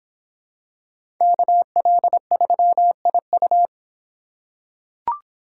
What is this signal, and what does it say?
Morse code, heard as a keyed tone, about 25 wpm, 700 Hz. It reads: KL3IU